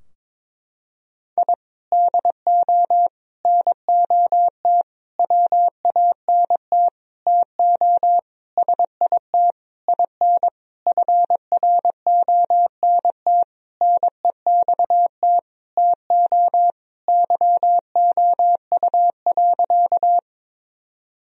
Morse code, heard as a keyed tone, about 22 wpm, 700 Hz. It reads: I DO NOT WANT TO SIT IN FRONT NEXT TO YOU.
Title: Sentences from Top 1000 Words - Review Part 2 - 22wpm